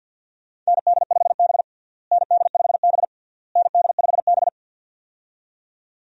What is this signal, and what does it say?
Morse code, heard as a keyed tone, about 53 wpm, 700 Hz. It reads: ND5B ND5B ND5B
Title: USA Call Signs - 4 Characters - Repeated 3x - 50wpm